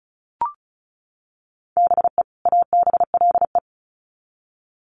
Morse code, heard as a keyed tone, about 34 wpm, 700 Hz. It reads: BE ABLE
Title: Sets of 2 Words 35wpm